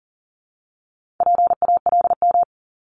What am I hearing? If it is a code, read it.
Morse code, heard as a keyed tone, about 40 wpm, 700 Hz. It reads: PALM
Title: Multi-letter Phonemes Repeated 3x 40wpm